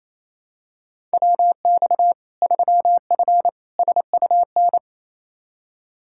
Morse code, heard as a keyed tone, about 28 wpm, 700 Hz. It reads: WX 3F SUN
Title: Practice QSO - Exchange 2 - Info - 28wpm